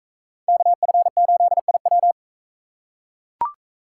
Morse code, heard as a keyed tone, about 41 wpm, 700 Hz. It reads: KF8IW